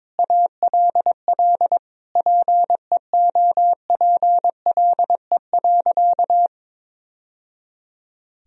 Morse code, heard as a keyed tone, about 22 wpm, 700 Hz. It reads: ALL PEOPLE.
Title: Sentences from Top 1000 Words - Review Part 3 - 22wpm